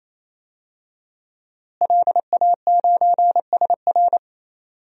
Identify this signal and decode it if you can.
Morse code, heard as a keyed tone, about 28 wpm, 700 Hz. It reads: LA9SR
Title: DXCC-100 Call Signs - 5 Characters - Repeated 3x - 28wpm